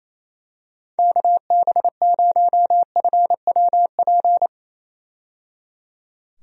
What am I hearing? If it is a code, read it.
Morse code, heard as a keyed tone, about 28 wpm, 700 Hz. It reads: KB0FWP
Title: USA Call Signs - 6 Characters - 28wpm